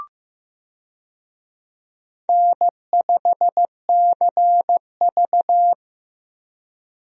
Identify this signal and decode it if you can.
Morse code, heard as a keyed tone, about 16 wpm, 700 Hz. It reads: N5CV
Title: USA Call Signs - 4 Characters - Repeated 3x - 15wpm